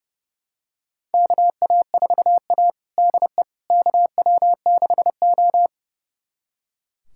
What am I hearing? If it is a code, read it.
Morse code, heard as a keyed tone, about 30 wpm, 700 Hz. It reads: KA4A DE KW6O
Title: USA Callsign DE Callsign - 4 Characters - 30wpm